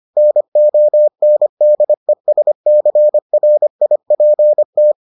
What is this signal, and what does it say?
Morse code, heard as a keyed tone, about 25 wpm, 600 Hz. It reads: NONDESCRIPT